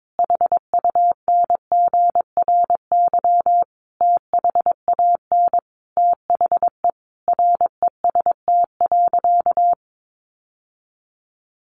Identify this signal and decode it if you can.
Morse code, heard as a keyed tone, about 22 wpm, 700 Hz. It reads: HUNGRY THAN THE REST.
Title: Sentences from Top 1400 Words 22wpm